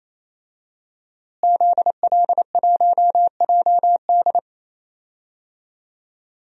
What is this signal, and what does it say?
Morse code, heard as a keyed tone, about 28 wpm, 700 Hz. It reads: ZL1JD
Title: DXCC-100 Call Signs - 5 Characters - Repeated 3x - 28wpm